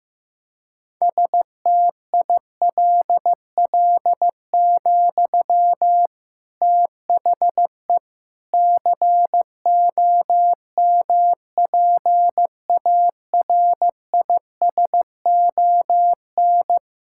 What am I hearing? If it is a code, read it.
Morse code, heard as a keyed tone, about 16 wpm, 700 Hz. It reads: STILL, THE COMPARISON
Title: Sentences from Top 1600 Words 15wpm